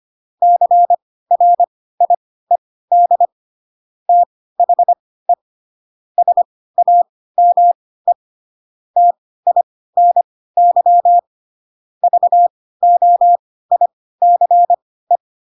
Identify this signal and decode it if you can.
Morse code, heard as a keyed tone, about 25 wpm, spaced as for 16 wpm, 700 Hz. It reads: CRIED THE SAME TINY VOICE
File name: Pinocchio / Pinocchio ch2 10 25 ik1hge